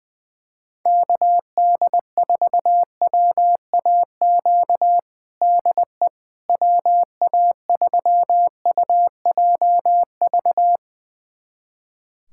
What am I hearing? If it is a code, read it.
Morse code, heard as a keyed tone, about 21 wpm, 700 Hz. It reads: KD4WAQ DE WA3UJV
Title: USA Callsign DE Callsign 20wpm